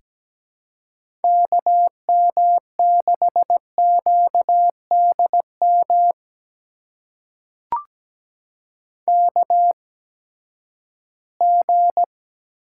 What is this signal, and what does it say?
Morse code, heard as a keyed tone, about 17 wpm, spaced as for 12 wpm, 700 Hz. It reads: KM6QDM KG
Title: USA Call Signs - 6 Characters - 8x Character Spacing - 17wpm